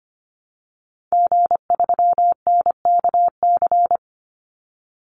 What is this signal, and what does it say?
Morse code, heard as a keyed tone, about 25 wpm, 700 Hz. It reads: G3NKC
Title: Call Signs Repeated 3x - 25wpm